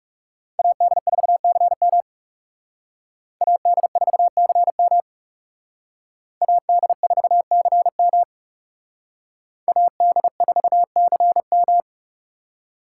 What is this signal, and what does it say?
Morse code, heard as a keyed tone, about 39 wpm, 700 Hz. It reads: AD4CM AD4CM AD4CM AD4CM